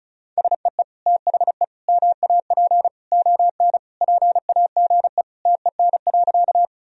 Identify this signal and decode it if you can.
Morse code, heard as a keyed tone, about 36 wpm, 700 Hz. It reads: SEE THE MAP ON PAGE TEN.
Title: Sentences from Top 1000 Words - Review Part 2 - 35wpm